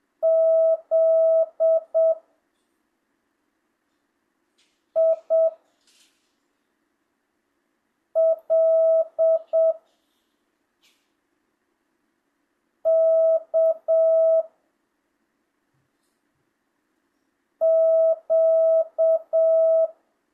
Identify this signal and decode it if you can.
Morse code, heard as a keyed tone, about 7 wpm, 660 Hz. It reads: Z I L K Q
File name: jh8cbh / 2025cw